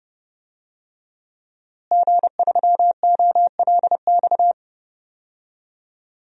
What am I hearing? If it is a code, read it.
Morse code, heard as a keyed tone, about 30 wpm, 700 Hz. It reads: G3OLX
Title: DXCC-100 Call Signs - 5 Characters - Repeated 3x - 30wpm